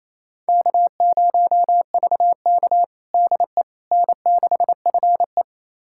Morse code, heard as a keyed tone, about 28 wpm, 700 Hz. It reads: K0VK DE N6FE
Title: USA Callsign DE Callsign - 4 Characters - 28wpm